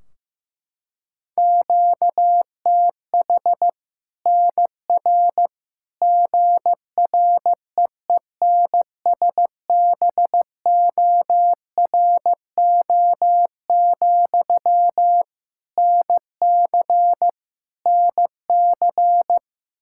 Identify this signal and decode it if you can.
Morse code, heard as a keyed tone, about 15 wpm, 700 Hz. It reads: QTH NR GREENSBORO, NC NC
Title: Practice QSO - Exchange 1 - Beginning - 15wpm